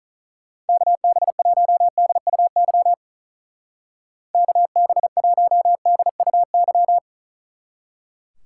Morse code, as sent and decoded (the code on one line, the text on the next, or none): -.- -... .---- -.. ..- -.-- / -.- -... .---- -.. ..- -.--
KB1DUY KB1DUY